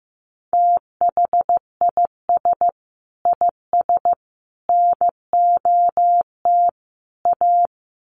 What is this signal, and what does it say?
Morse code, heard as a keyed tone, about 15 wpm, 700 Hz. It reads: THIS IS NOT A
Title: Sets of 4 Words 15wpm